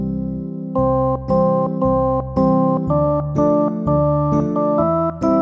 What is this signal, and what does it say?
There is no code.